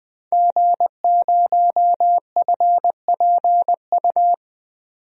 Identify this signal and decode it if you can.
Morse code, heard as a keyed tone, about 20 wpm, 700 Hz. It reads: G0FPU
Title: DXCC-100 Call Signs - 5 Characters - 20wpm